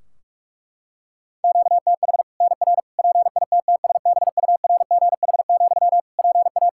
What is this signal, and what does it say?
Morse code, heard as a keyed tone, about 46 wpm, 700 Hz. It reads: QTH NR PITTSBURGH, PA
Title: Practice QSO - Exchange 1 - Beginning - 45wpm